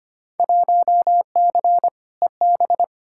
Morse code, heard as a keyed tone, about 25 wpm, 700 Hz. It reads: 1C EB
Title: Field Day Exchanges Encore - 25wpm